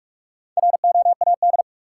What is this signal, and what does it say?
Morse code, heard as a keyed tone, about 46 wpm, 700 Hz. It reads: ROAD
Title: Sentences from Top 400 Words - New Word Drill - 45wpm